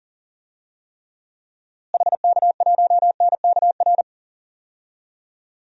Morse code, heard as a keyed tone, about 40 wpm, 700 Hz. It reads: HK1NKR
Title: DXCC-100 Call Signs - 6 Characters - Repeated 2x - 40wpm